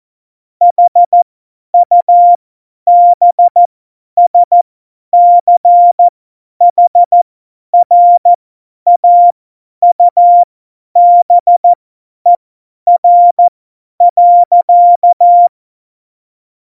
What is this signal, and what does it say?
Morse code, heard as a keyed tone, about 14 wpm, spaced as for 10 wpm, 700 Hz. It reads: HUBSCHRAUBER.
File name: Morsekurs / Witze / Witz 22. August 17, Tempo 50 F70